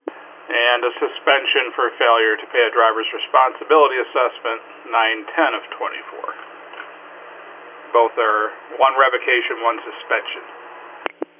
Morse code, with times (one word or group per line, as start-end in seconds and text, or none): none